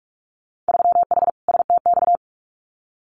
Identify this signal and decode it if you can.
Morse code, heard as a keyed tone, about 46 wpm, 700 Hz. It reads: 3H STX
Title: Winter Field Day Exchanges 45wpm